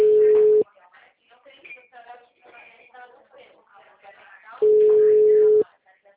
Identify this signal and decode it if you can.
Ringback tone (Europe, 425 Hz), with speech.